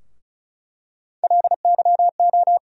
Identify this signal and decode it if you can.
Morse code, heard as a keyed tone, about 34 wpm, 700 Hz. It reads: LYO
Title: USA Call Sign Postfixes - 3 Letters 35wpm